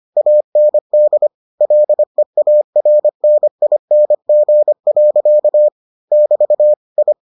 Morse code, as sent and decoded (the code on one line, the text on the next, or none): .- -. -.. / .-.. .- .-. -. .. -. --. .-.-.- / -...- / ..
AND LEARNING. <BT> I